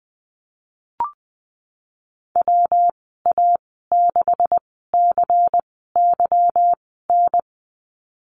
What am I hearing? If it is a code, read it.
Morse code, heard as a keyed tone, about 20 wpm, spaced as for 15 wpm, 700 Hz. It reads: WA6CYN
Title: 200 Lesson 32g - Letter k - Call Signs 20wpm